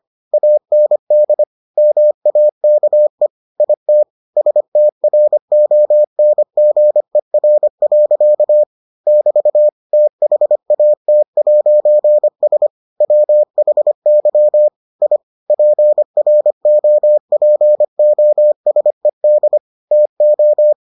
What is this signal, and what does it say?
Morse code, heard as a keyed tone, about 25 wpm, 600 Hz. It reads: AND MAKE IT STRONGER. <BT> THAT'S WHY I PROPOSED TO